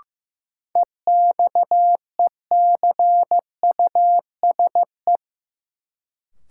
Morse code, heard as a keyed tone, about 15 wpm, 700 Hz. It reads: EXECUSE
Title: Sentences from Top 700 Words - New Word Drill - 15wpm